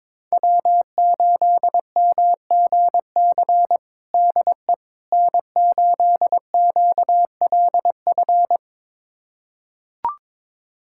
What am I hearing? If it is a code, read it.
Morse code, heard as a keyed tone, about 22 wpm, 700 Hz. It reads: W8MGC DE N8QLF
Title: USA Callsign DE Callsign - 5 Characters - 22wpm